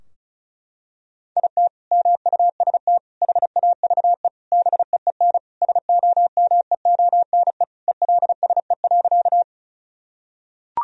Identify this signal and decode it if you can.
Morse code, heard as a keyed tone, about 36 wpm, 700 Hz. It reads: IT MUST HAVE BEEN SOMEONE ELSE.
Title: Sentences from Top 1000 Words - Review Part 4 - 35wpm